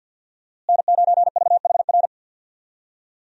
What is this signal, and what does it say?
Morse code, heard as a keyed tone, about 47 wpm, 700 Hz. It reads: N9VHR
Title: USA Call Signs - 5 Characters - Repeated 3x - 50wpm